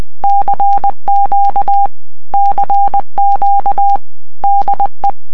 Morse code, heard as a keyed tone, about 20 wpm, 800 Hz. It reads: CQ CQ DE